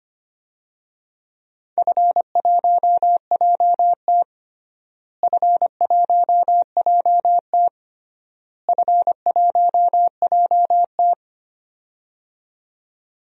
Morse code, heard as a keyed tone, about 25 wpm, 700 Hz. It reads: F1JT F1JT F1JT
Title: DXCC-100 Call Signs - 4 Characters - Repeated 3x - 25wpm